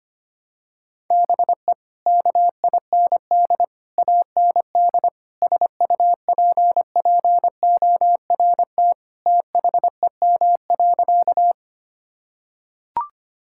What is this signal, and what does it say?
Morse code, heard as a keyed tone, about 25 wpm, 700 Hz. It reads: BE KIND AND SUPPORT THEM.
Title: Sentences from Top 1000 Words - Review Part 5 - 25wpm